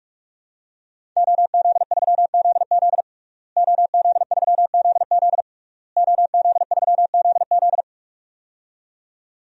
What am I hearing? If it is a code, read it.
Morse code, heard as a keyed tone, about 46 wpm, 700 Hz. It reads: OZ3ZZ OZ3ZZ OZ3ZZ
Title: DXCC-100 Call Signs - 5 Characters - Repeated 3x - 45wpm